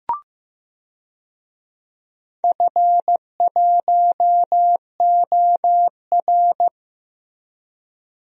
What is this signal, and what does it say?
Morse code, heard as a keyed tone, about 15 wpm, 700 Hz. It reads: F1OR
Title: DXCC-100 Call Signs - 4 Characters - Repeated 2x - 15wpm